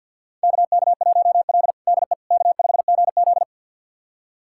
Morse code, heard as a keyed tone, about 48 wpm, 700 Hz. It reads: KK1L DE K5GZ
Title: USA Callsign DE Callsign - 4 Characters - 50wpm